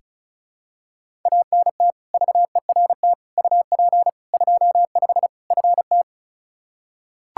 Morse code, heard as a keyed tone, about 36 wpm, 700 Hz. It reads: ANT VERT UP 25 FT
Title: Practice QSO - Exchange 2 - Info - 35wpm